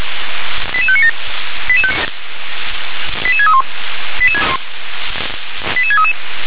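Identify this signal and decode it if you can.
Five-tone selective call (ZVEI), heard as quick runs of five tones, tones of about 70 ms. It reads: 88597 88631 88633
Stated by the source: ZVEI five-tone selective call received on an SDR.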